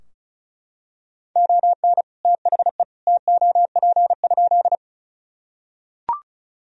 Morse code, heard as a keyed tone, about 36 wpm, 700 Hz. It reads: ON THE TOP?
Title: Sentences from Top 1000 Words - Review Part 2 - 35wpm